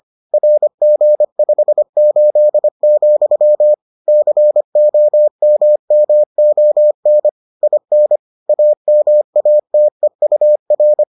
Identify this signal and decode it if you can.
Morse code, heard as a keyed tone, about 25 wpm, 600 Hz. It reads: RG58, COMMON IN AMATEUR